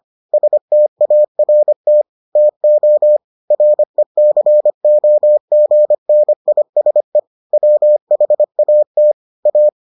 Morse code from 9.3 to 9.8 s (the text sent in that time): A